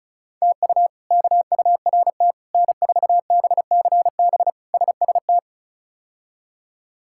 Morse code, heard as a keyed tone, about 36 wpm, 700 Hz. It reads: TU KURT N4BCB SST
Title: K1USN SST QSO Practice - 35wpm